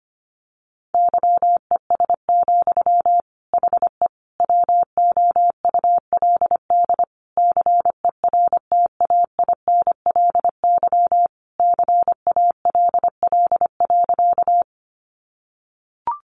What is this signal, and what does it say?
Morse code, heard as a keyed tone, about 25 wpm, 700 Hz. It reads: YES, HE WOULD CERTAINLY CALL.
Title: Sentences from Top 900 Words 25wpm